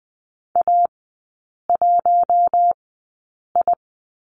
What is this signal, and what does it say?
Morse code, heard as a keyed tone, about 20 wpm, spaced as for 8 wpm, 700 Hz. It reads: A1I
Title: 074 Lesson 13e - Letter L - Three Characters 20wpm